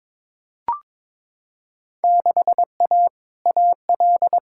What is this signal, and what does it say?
Morse code, heard as a keyed tone, about 22 wpm, 700 Hz. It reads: 6A AL
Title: Field Day Exchanges Encore - 22wpm